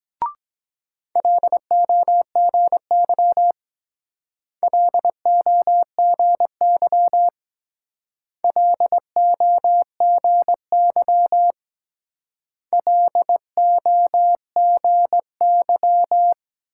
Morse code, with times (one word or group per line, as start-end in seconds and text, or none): 1.2-3.5 LOGY
4.6-7.3 LOGY
8.4-11.5 LOGY
12.7-16.3 LOGY